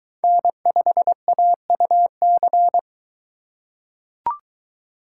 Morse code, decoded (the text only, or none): N5AUC